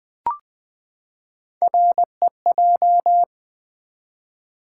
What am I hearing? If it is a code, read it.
Morse code, heard as a keyed tone, about 20 wpm, 700 Hz. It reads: REJ
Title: Build A Word - 3 Letters At A Time - 9-Letter Words - 20wpm